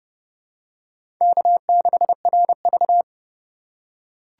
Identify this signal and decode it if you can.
Morse code, heard as a keyed tone, about 30 wpm, 700 Hz. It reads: K6RV
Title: USA Call Signs - 4 Characters - 3x Character Spacing - 30wpm